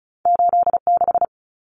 Morse code, heard as a keyed tone, about 36 wpm, 700 Hz. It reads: 86
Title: Common Cut Numbers - 2-Digit 35wpm